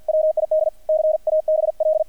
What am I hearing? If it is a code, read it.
Morse code, heard as a keyed tone, about 50 wpm, 620 Hz, with radio noise. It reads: WID CABL
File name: archived / CRSnet